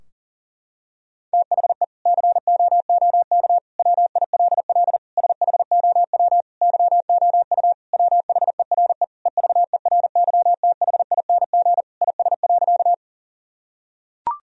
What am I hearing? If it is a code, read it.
Morse code, heard as a keyed tone, about 40 wpm, 700 Hz. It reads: THE COOK WILL SHOW YOU WHERE EVERYTHING IS.
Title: Sentences from Top 1000 Words - Review Part 4 - 40wpm